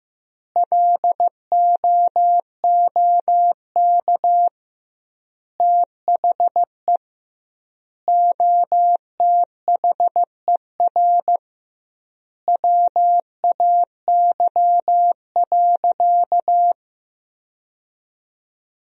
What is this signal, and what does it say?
Morse code, heard as a keyed tone, about 16 wpm, 700 Hz. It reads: LOOK THE OTHER WAY.
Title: 2x Word-Spacing - Sentences from Top 100 Words - 15wpm